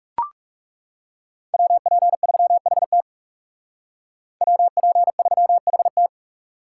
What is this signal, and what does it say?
Morse code, heard as a keyed tone, about 35 wpm, 700 Hz. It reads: WP3HT WP3HT
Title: Speed Racing - USA 2x2 Call Signs 30wpm